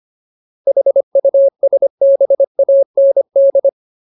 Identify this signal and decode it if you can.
Morse code, heard as a keyed tone, about 25 wpm, 550 Hz. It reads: HUSBAND
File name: Ultimate Word List / 25WPM Ultimate